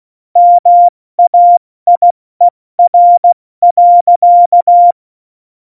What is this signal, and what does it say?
Morse code, heard as a keyed tone, about 16 wpm, 700 Hz. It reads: MAIER.
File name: Morsekurs / Witze / Witz 22. September 2020, 70 F80